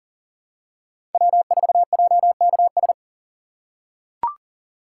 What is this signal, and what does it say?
Morse code, heard as a keyed tone, about 40 wpm, 700 Hz. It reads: W4JKS
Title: USA Call Signs - 5 Characters - 6x Character Spacing - 40wpm